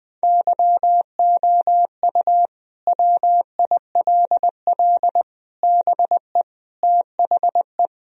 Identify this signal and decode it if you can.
Morse code, heard as a keyed tone, about 21 wpm, 700 Hz. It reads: YOU WILL BE THE
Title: Sentences from Top 1000 Words - Review Part 4 - 20wpm